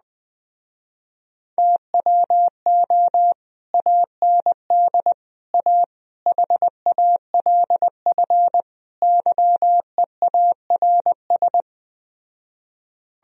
Morse code, heard as a keyed tone, about 20 wpm, 700 Hz. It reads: TWO AND A HALF YEARS
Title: Speed-Racing - Sets of 5 Words 17wpm